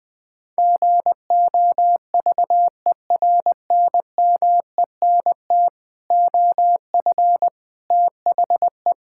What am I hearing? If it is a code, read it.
Morse code, heard as a keyed tone, about 20 wpm, 700 Hz. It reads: GOVERNMENT OF THE
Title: Sets of 9 Words - 20wpm